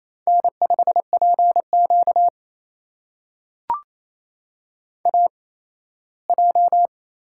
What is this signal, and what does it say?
Morse code, heard as a keyed tone, about 28 wpm, spaced as for 17 wpm, 700 Hz. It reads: N5PQ AJ
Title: USA Call Signs - 4 Characters - 8x Character Spacing - 28wpm